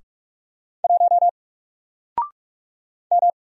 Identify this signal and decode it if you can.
Morse code, heard as a keyed tone, about 44 wpm, 700 Hz. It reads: 1 M
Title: Single Character (Letter:Number:Punctuation:Prosign) 45wpm